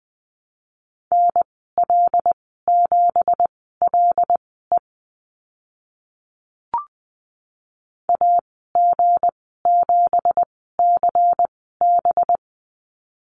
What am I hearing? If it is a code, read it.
Morse code, heard as a keyed tone, about 20 wpm, spaced as for 14 wpm, 700 Hz. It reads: NL7LE AG7CB
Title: 213 Lesson 34g - Number 8 - Call Signs 20wpm